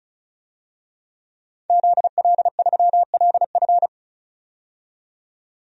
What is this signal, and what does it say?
Morse code, heard as a keyed tone, about 36 wpm, 700 Hz. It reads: ZL3LF
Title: DXCC-100 Call Signs - 5 Characters - 35wpm